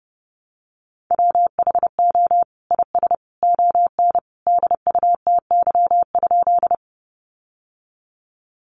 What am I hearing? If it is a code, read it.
Morse code, heard as a keyed tone, about 28 wpm, 700 Hz. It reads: WHO IS ON DUTY?
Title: Sentences from Top 1500 Words 30wpm